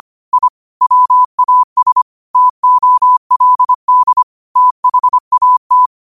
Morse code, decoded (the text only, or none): I WAS TOLD THAT